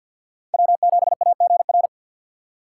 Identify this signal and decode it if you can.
Morse code, heard as a keyed tone, about 51 wpm, 700 Hz. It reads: W7AGR
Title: USA Call Signs - 5 Characters - 2x Character Spacing - 50wpm